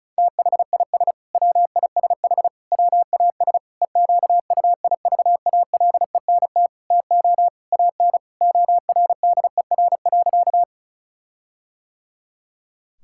Morse code, heard as a keyed tone, about 34 wpm, 700 Hz. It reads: THIS WISH WAS EQUIVALENT TO AN ORDER.
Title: Sentences from Top 1600 Words 35wpm